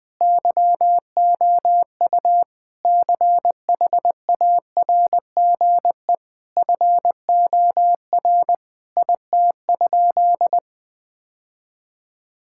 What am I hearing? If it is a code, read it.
Morse code, heard as a keyed tone, about 20 wpm, 700 Hz. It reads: YOU CHARGE FOR IT?
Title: Sentences from Top 1000 Words - Review Part 6 - 20wpm